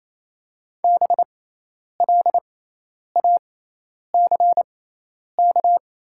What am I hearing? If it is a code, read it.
Morse code, heard as a keyed tone, about 28 wpm, spaced as for 10 wpm, 700 Hz. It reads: BLACK
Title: 6x Character Spacing - Top 500 Words - 28wpm